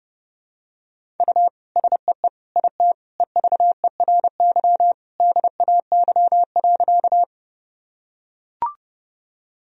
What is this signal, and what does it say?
Morse code, heard as a keyed tone, about 30 wpm, 700 Hz. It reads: U SEE IT EVERY DAY.
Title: Sentences from Top 200 Words and Abbreviations 30wpm